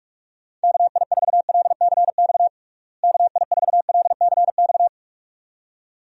Morse code, heard as a keyed tone, about 46 wpm, 700 Hz. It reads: KI4LCX KI4LCX
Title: USA Call Signs - 6 Characters - Repeated 2x - 45wpm